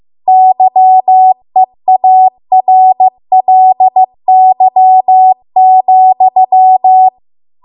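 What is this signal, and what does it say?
Morse code, heard as a keyed tone, about 15 wpm, 750 Hz. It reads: YEARLY,
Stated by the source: Morse code practice at 15 wpm from the national amateur radio association's code archive.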